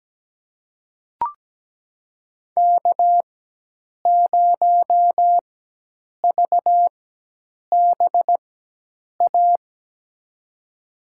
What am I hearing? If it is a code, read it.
Morse code, heard as a keyed tone, about 17 wpm, spaced as for 8 wpm, 700 Hz. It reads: K0VBA
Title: USA Call Signs - 5 Characters - 4x Character Spacing - 17wpm